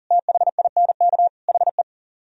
Morse code, heard as a keyed tone, about 52 wpm, 700 Hz. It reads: THINK HE